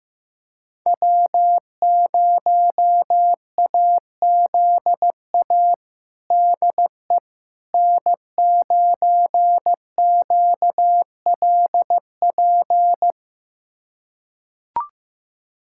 Morse code, heard as a keyed tone, about 15 wpm, 700 Hz. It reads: W0AZA DE N9QLP
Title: USA Callsign DE Callsign - 5 Characters - 15wpm